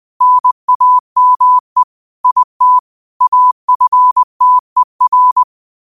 Morse code, heard as a keyed tone, about 20 wpm, 1000 Hz. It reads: NAME IT AFTER
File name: qotdfiles / 20wpm